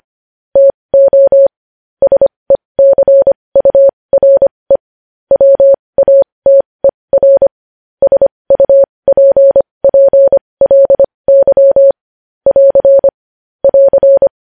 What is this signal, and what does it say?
Morse code, heard as a keyed tone, about 25 wpm, spaced as for 20 wpm, 550 Hz. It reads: TO SECURE WATER SUPPLY <AR> <AR>
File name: podcast / News headlines-1x-25x20